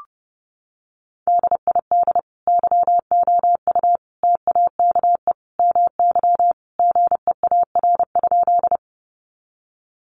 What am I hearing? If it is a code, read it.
Morse code, heard as a keyed tone, about 30 wpm, 700 Hz. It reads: DID YOU TAKE MY GEAR?
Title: Sentences from Top 1400 Words 30wpm